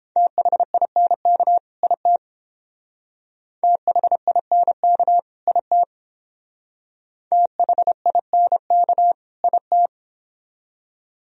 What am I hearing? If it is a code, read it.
Morse code, heard as a keyed tone, about 29 wpm, 700 Hz. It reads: THINK IT THINK IT THINK IT